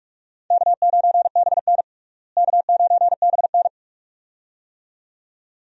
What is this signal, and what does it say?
Morse code, heard as a keyed tone, about 46 wpm, 700 Hz. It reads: K9BN K9BN
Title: USA Call Signs - 4 Characters - Repeated 2x - 45wpm